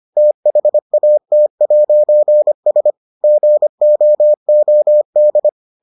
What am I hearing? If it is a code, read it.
Morse code, heard as a keyed tone, about 25 wpm, 600 Hz. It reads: THAT'S GOOD